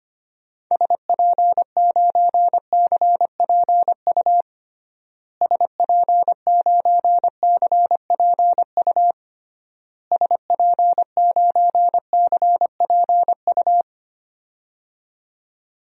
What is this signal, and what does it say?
Morse code, heard as a keyed tone, about 25 wpm, 700 Hz. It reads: SP9CPU SP9CPU SP9CPU
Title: DXCC-100 Call Signs - 6 Characters - Repeated 3x - 25wpm